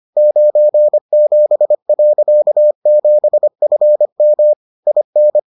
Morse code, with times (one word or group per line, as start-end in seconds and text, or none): 0.2-4.5 97.7FM
4.9-5.4 IN